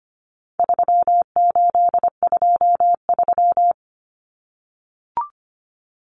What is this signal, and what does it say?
Morse code, heard as a keyed tone, about 25 wpm, 700 Hz. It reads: 3823
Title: Common Cut Numbers - 4-Digit 25wpm